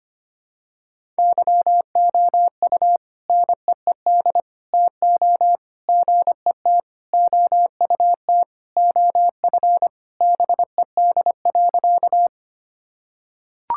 Morse code, heard as a keyed tone, about 25 wpm, 700 Hz. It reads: YOU NEED TO GET OUT OF BED.
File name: practice / 25wpm / Sentences with 3-Word Phrases - 25wpm